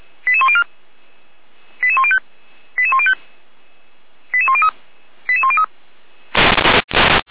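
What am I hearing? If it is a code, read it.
Five-tone selective call (ZVEI), heard as quick runs of five tones, tones of about 70 ms. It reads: 88104 88105 88105 88103 88103
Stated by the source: ZVEI five-tone call received by SDR.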